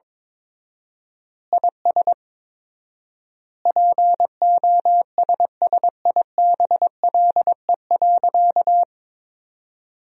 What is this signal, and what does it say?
Morse code, heard as a keyed tone, about 21 wpm, 700 Hz. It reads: IS POSSIBLE.
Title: 4x Word-Spacing - Sentences from Top 500 Words - 22wpm